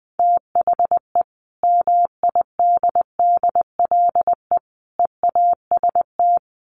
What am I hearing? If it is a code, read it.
Morse code, heard as a keyed tone, about 20 wpm, 700 Hz. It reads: THE MIDDLE EAST
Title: Sets of 3 Words 20wpm